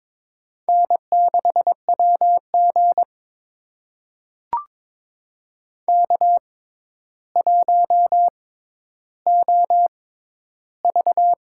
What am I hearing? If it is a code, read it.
Morse code, heard as a keyed tone, about 22 wpm, spaced as for 11 wpm, 700 Hz. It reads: N6WG K1OV
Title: USA Call Signs - 4 Characters - 6x Character Spacing - 22wpm